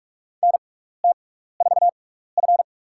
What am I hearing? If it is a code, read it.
Morse code, heard as a keyed tone, about 46 wpm, spaced as for 16 wpm, 700 Hz. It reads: NT4F
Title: USA Call Signs - 4 Characters - 6x Character Spacing - 45wpm